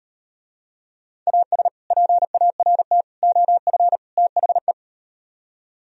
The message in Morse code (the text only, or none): AS PART OF THE